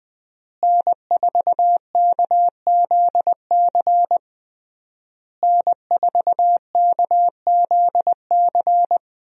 Morse code, heard as a keyed tone, about 20 wpm, 700 Hz. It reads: N4KZC N4KZC
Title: USA Call Signs - 5 Characters - Repeated 3x - 20wpm